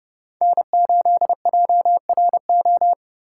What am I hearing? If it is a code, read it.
Morse code, heard as a keyed tone, about 34 wpm, 700 Hz. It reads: N8JRO
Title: Speed Racing - USA 1x3 Call Signs 30wpm